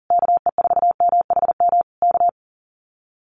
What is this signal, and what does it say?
Morse code, heard as a keyed tone, about 40 wpm, 700 Hz. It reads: KE4MHM K
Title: Common QSO Phrases Encore 40wpm